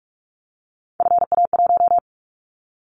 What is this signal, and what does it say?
Morse code, heard as a keyed tone, about 46 wpm, 700 Hz. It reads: FA1
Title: Callsign Prefixes for Top 50 DXCC - 45wpm